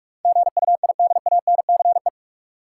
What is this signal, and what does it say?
Morse code, heard as a keyed tone, about 42 wpm, 700 Hz. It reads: GUIDANCE
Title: Sentences from Top 1600 Words - New Word Drill - 45wpm